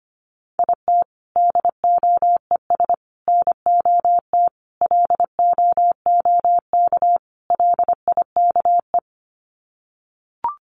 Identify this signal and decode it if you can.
Morse code, heard as a keyed tone, about 25 wpm, 700 Hz. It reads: IT DOES NOT LOOK LIKE
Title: Sets of 5 Words Encore2 25wpm